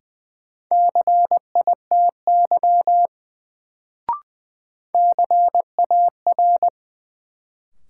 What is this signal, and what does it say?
Morse code, heard as a keyed tone, about 20 wpm, 700 Hz. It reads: CITY CAR
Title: Sentences from Top 300 Words - New Word Drill - 20wpm